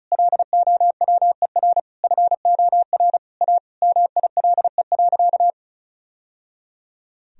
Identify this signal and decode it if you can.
Morse code, heard as a keyed tone, about 36 wpm, 700 Hz. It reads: LOWER FOR A MILE.
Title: Sentences from Top 1000 Words - Review Part 4 - 35wpm